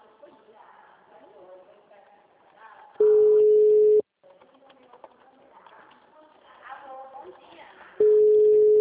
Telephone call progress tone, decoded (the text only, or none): ringback tone